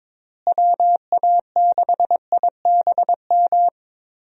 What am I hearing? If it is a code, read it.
Morse code, heard as a keyed tone, about 22 wpm, 700 Hz. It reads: WA6IBM
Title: USA Call Signs - 6 Characters - Repeated 2x - 22wpm